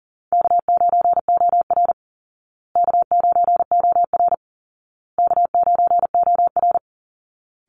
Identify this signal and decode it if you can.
Morse code, heard as a keyed tone, about 40 wpm, 700 Hz. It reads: K9OR K9OR K9OR
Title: Call Signs Repeated 2x - 40wpm